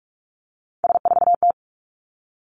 Morse code, heard as a keyed tone, about 46 wpm, 700 Hz. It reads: S4T
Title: Three Challenging Letter-Number Combinations 45wpm